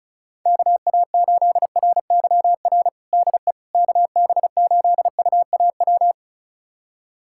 Morse code, heard as a keyed tone, about 34 wpm, 690 Hz. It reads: KA8RYR DE KB8UAW
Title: USA Callsign DE Callsign 35wpm